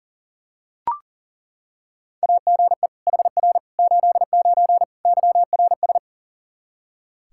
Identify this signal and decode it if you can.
Morse code, heard as a keyed tone, about 37 wpm, 700 Hz. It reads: AGE HR 89 YRS
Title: Practice QSO - Exchange 3 - More Info - 40wpm